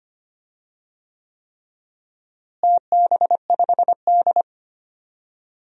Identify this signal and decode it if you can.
Morse code, heard as a keyed tone, about 25 wpm, 700 Hz. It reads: TB5D
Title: DXCC-100 Call Signs - 4 Characters - Repeated 3x - 25wpm